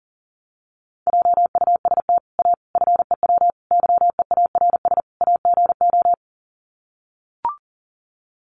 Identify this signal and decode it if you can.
Morse code, heard as a keyed tone, about 40 wpm, 700 Hz. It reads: JUST A FEW YEARS AGO
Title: Sets of 5 Words Encore 40wpm